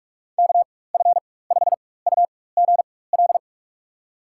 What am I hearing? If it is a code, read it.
Morse code, heard as a keyed tone, about 41 wpm, spaced as for 20 wpm, 700 Hz. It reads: KF5UGL
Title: USA Call Signs - 6 Characters - 4x Character Spacing - 45wpm